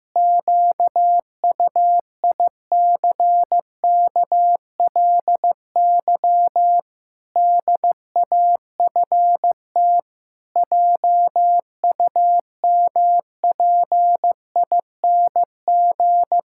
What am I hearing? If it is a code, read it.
Morse code, heard as a keyed tone, about 15 wpm, 700 Hz. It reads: QUICKLY DAFT JUMPING
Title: Pangram Challenge v2 - 15wpm